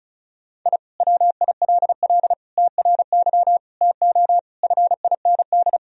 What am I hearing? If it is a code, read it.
Morse code, heard as a keyed tone, about 34 wpm, 700 Hz. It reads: I WILL TRY TO FIND
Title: Sentences from Top 1000 Words - Review Part 4 - 35wpm